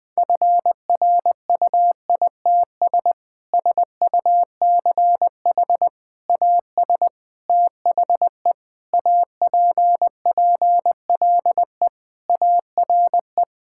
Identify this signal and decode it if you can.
Morse code, heard as a keyed tone, about 20 wpm, 700 Hz. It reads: FRUITS SUCH AS THE APPLE ARE
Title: Sentences with 3-Word Phrases - 20wpm